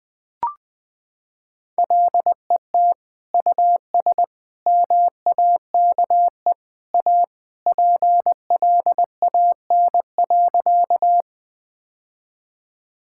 Morse code, heard as a keyed tone, about 20 wpm, 700 Hz. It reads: LET US MAKE A PLAN.
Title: Sentences with 2-Word Phrases - 20wpm